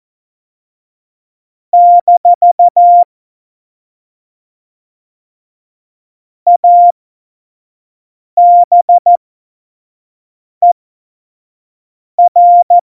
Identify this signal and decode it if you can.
Morse code, heard as a keyed tone, about 14 wpm, spaced as for 5 wpm, 700 Hz. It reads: - ABER
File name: Morsekurs / Witze / Witz 22. Oktober 2019, 25 F70